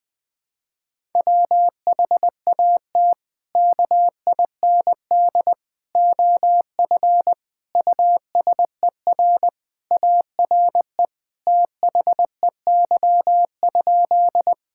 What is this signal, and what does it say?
Morse code, heard as a keyed tone, about 21 wpm, 700 Hz. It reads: WHAT KIND OF USER ARE THEY?
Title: Sentences from Top 1000 Words - Review Part 5 - 20wpm